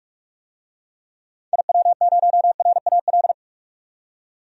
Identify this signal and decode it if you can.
Morse code, heard as a keyed tone, about 46 wpm, 700 Hz. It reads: IW0RAL
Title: DXCC-100 Call Signs - 6 Characters - 45wpm